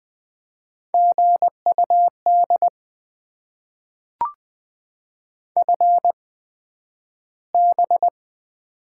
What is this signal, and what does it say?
Morse code, heard as a keyed tone, about 20 wpm, spaced as for 11 wpm, 700 Hz. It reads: GUD FB
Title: Top 20 QSO Elements - 8x Character Spacing - 20wpm